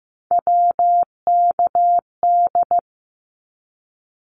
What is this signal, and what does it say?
Morse code, heard as a keyed tone, about 15 wpm, 700 Hz. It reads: WKD
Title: Abbreviations 15wpm